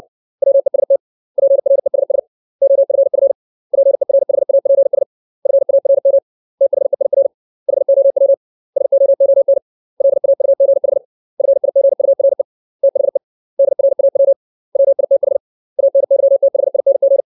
Morse code, heard as a keyed tone, about 60 wpm, 550 Hz. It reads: WEST PRESS OFF PERHAPS FARM THEIR HOW SOON LAUGH FIGURE THE DRAW WITH ANYTHING
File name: 632 words / 632 Words 60WPM